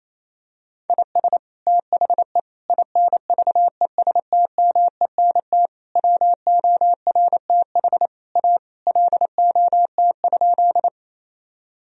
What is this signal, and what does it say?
Morse code, heard as a keyed tone, about 28 wpm, 700 Hz. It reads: IS THE INVESTMENT WORTH A LOT?
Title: Sentences from Top 1000 Words - Review Part 4 - 28wpm